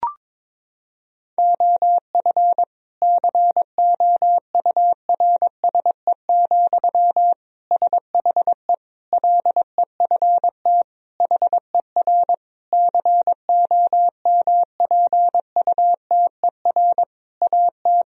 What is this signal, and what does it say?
Morse code, heard as a keyed tone, about 22 wpm, 700 Hz. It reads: OF COURSE, SHE LEFT HER COMPUTER AT
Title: Sentences from Top 1000 Words - Review Part 3 - 22wpm